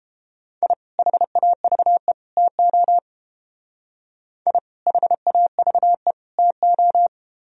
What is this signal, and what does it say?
Morse code, heard as a keyed tone, about 31 wpm, 700 Hz. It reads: I HAVE TO I HAVE TO